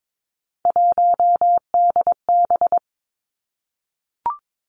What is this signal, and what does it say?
Morse code, heard as a keyed tone, about 22 wpm, 700 Hz. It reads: 1DB
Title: Three Challenging Letter-Number Combinations 22wpm